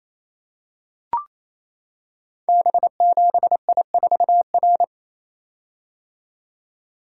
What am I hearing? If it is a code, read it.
Morse code, heard as a keyed tone, about 27 wpm, 700 Hz. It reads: B7I4R
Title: Code Groups - 5 Letter:Numbers - 28wpm